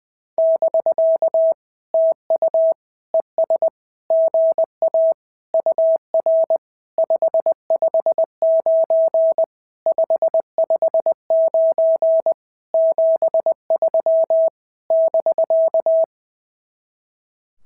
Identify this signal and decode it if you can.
Morse code, heard as a keyed tone, about 20 wpm, 650 Hz. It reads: <BK> TU ES GA UR 559 559 73 <BK>
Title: SOTA QSO Practice 20wpm